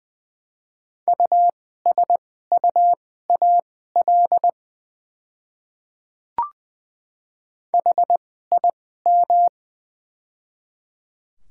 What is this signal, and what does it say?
Morse code, heard as a keyed tone, about 20 wpm, spaced as for 14 wpm, 700 Hz. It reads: USUAL HIM
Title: 2x Character Spacing - Top 500 Words - 20wpm